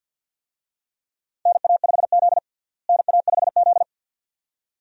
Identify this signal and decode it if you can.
Morse code, heard as a keyed tone, about 49 wpm, 700 Hz. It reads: NA5Z NA5Z
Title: USA Call Signs - 4 Characters - Repeated 2x - 50wpm